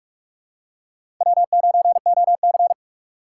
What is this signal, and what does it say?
Morse code, heard as a keyed tone, about 45 wpm, 700 Hz. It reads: W9OC
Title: USA Call Signs - 6x Character Spacing - 45wpm